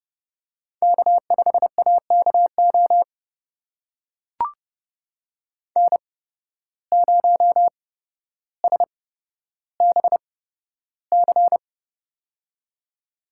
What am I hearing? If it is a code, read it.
Morse code, heard as a keyed tone, about 30 wpm, spaced as for 13 wpm, 700 Hz. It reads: K5AKO N0SBC
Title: USA Call Signs - 8x Character Spacing - 30wpm